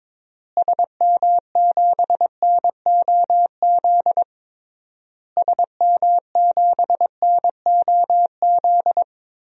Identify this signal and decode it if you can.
Morse code, heard as a keyed tone, about 22 wpm, 700 Hz. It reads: SM7NOZ SM7NOZ
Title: DXCC-100 Call Signs - 6 Characters - Repeated 3x - 22wpm